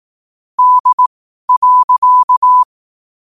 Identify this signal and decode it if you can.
Morse code, heard as a keyed tone, about 18 wpm, spaced as for 12 wpm, 1000 Hz. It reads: D.
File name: qotdfiles / 12wpm